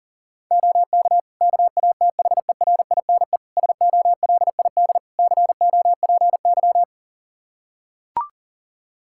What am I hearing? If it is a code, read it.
Morse code, heard as a keyed tone, about 40 wpm, 700 Hz. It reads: OK KATHERINE SOLID COPY
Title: Practice QSO - Exchange 3 - More Info - 40wpm